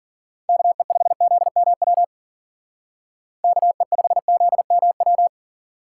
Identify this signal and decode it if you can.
Morse code, heard as a keyed tone, about 48 wpm, 700 Hz. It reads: KE5ZMW KE5ZMW
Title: Speed Racing - USA 2x3 Call Signs 40wpm